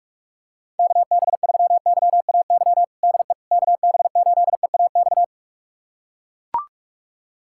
Morse code, heard as a keyed tone, about 41 wpm, 700 Hz. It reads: KB3YAY DE KB8EAX
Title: USA Callsign DE Callsign - 6 Characters - 45wpm